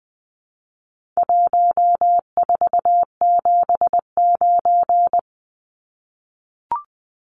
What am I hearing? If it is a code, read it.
Morse code, heard as a keyed tone, about 20 wpm, 700 Hz. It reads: 1479
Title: Numbers - Contest Style - 4-Digit 20wpm